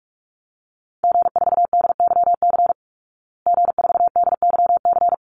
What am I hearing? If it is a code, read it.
Morse code, heard as a keyed tone, about 44 wpm, 700 Hz. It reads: G4DYC G4DYC
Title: Call Signs Repeated 3x - 45wpm